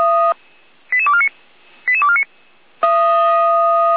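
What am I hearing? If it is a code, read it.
Five-tone selective call (ZVEI), heard as quick runs of five tones, tones of about 70 ms. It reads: 88249 88249
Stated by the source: ZVEI five-tone call 88247 received with an SDR, recorded on the Feuerwehr Orlamünde server.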